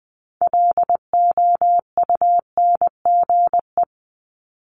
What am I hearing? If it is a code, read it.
Morse code, heard as a keyed tone, about 20 wpm, 700 Hz. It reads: LOUNGE
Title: Multi-letter Phonemes 20wpm